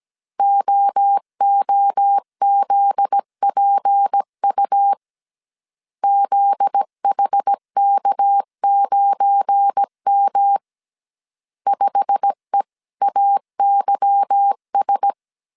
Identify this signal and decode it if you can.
Morse code, heard as a keyed tone, about 17 wpm, 800 Hz. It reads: OOZPU ZHK9M 5EAYS
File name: CW / DET / 1 9 1500C S A 16wpm 2828